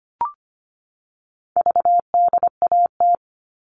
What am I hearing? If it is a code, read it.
Morse code, heard as a keyed tone, about 25 wpm, 700 Hz. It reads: VDAT
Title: All Cut Numbers - 4-Digit 25wpm